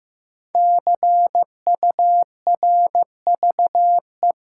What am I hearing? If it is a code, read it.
Morse code, heard as a keyed tone, about 15 wpm, 700 Hz. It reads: CURVE
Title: Five Letter Words - 15wpm